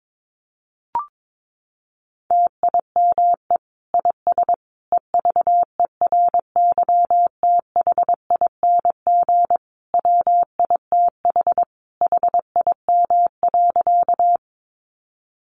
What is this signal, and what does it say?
Morse code, heard as a keyed tone, about 22 wpm, 700 Hz. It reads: TIME IS EVERYTHING WITH HIM.
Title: Sentences from Top 700 Words 22wpm